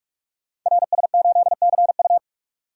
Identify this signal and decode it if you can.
Morse code, heard as a keyed tone, about 44 wpm, 690 Hz. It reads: RS8CU
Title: Code Groups - 5 Letter:Numbers - 45wpm